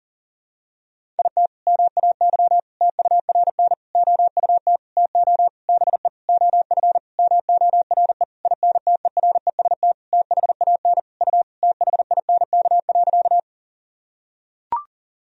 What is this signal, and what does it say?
Morse code, heard as a keyed tone, about 40 wpm, 700 Hz. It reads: IT MAY TURN OUT TO BE OF MORE INTEREST THAN U THINK.
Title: Sentences from Top 400 Words and Abbreviations 40wpm